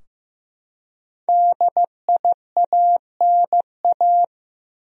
Morse code, heard as a keyed tone, about 15 wpm, 700 Hz. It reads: DIANA
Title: USA Common Names - 15wpm